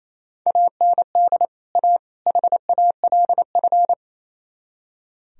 Morse code, heard as a keyed tone, about 37 wpm, 700 Hz. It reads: AND A HALF